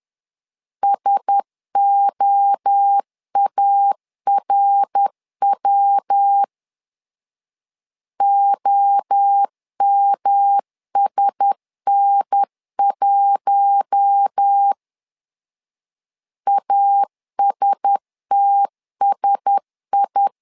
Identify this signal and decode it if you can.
Morse code, heard as a keyed tone, about 10 wpm, 800 Hz. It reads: SOARW OMSN1 ASTSI
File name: LEZIONI / 4 Q1S 10wpm 2133